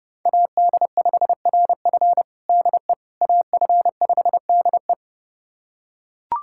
Morse code, heard as a keyed tone, about 30 wpm, 700 Hz. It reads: AD5RF DE AF5DE
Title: USA Callsign DE Callsign - 5 Characters - 30wpm